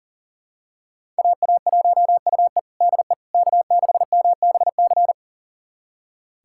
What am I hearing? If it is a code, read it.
Morse code, heard as a keyed tone, about 40 wpm, 700 Hz. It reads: AA1UE DE K6MBC
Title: USA Callsign DE Callsign - 5 Characters - 40wpm